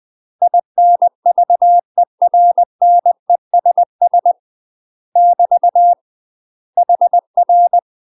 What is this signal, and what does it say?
Morse code, heard as a keyed tone, about 20 wpm, 700 Hz. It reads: INVERNESS <BT> HR
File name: UK MASTER QSOs TRAINING/20WPM